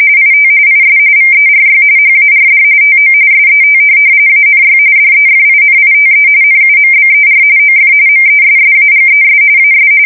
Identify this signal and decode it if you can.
RTTY, about 45.45 baud. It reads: IT IS VERY HARD TO SET ANY DEFINITE NUMBER OF MESSAGES AS
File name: fsk170 / printing-telegraph